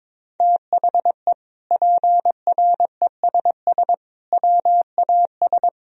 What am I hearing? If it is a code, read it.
Morse code, heard as a keyed tone, about 22 wpm, 700 Hz. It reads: THE PRESS WAS
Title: Sentences from Top 1000 Words - Review Part 1 - 22wpm